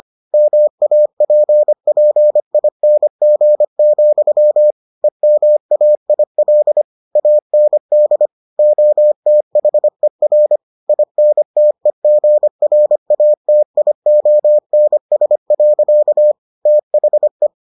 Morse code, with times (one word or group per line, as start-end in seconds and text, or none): 0.3-4.7 MAPPING,
5.0-6.8 EMAIL
7.2-8.3 AND
8.6-10.6 OTHER
10.9-16.3 INTEGRATIONS.
16.7-17.5 THE